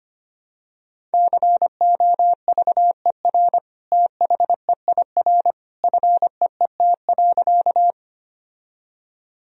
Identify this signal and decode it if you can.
Morse code, heard as a keyed tone, about 25 wpm, 700 Hz. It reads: COVER THEIR FEET.
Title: Sentences from Top 1000 Words - Review Part 1 - 25wpm